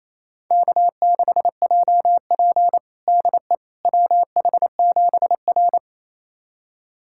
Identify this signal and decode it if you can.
Morse code, heard as a keyed tone, about 28 wpm, 700 Hz. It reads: K6JP DE WH7R
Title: USA Callsign DE Callsign - 4 Characters - 28wpm